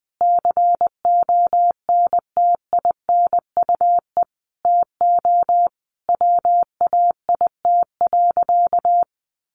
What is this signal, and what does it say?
Morse code, heard as a keyed tone, about 20 wpm, 700 Hz. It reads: CONTINUE TO WAIT.
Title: Easy Phrases 20wpm